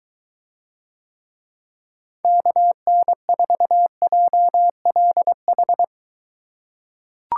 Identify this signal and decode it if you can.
Morse code, heard as a keyed tone, about 23 wpm, 700 Hz. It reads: KN4JLH